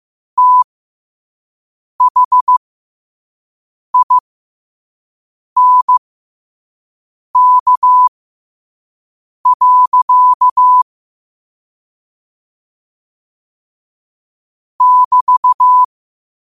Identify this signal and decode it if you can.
Morse code, heard as a keyed tone, about 15 wpm, spaced as for 5 wpm, 1000 Hz. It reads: THINK. <BT>
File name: qotdfiles / 5wpm